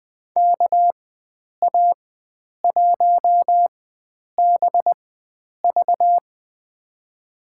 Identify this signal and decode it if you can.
Morse code, heard as a keyed tone, about 20 wpm, spaced as for 9 wpm, 700 Hz. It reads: KA1BV
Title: USA Call Signs - 5 Characters - 4x Character Spacing - 20wpm